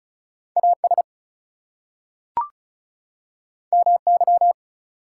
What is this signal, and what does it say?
Morse code, heard as a keyed tone, about 36 wpm, 700 Hz. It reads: AS MY
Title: Two Letter Words - 35wpm